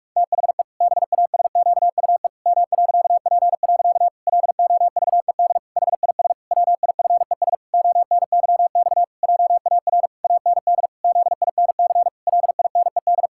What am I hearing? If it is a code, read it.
Morse code, heard as a keyed tone, about 46 wpm, 700 Hz. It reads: THE BASQUE M.P. LOVED HIS WIFES ONYX JAR AND ZINC LINED
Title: Pangram Challenge v2 - 45wpm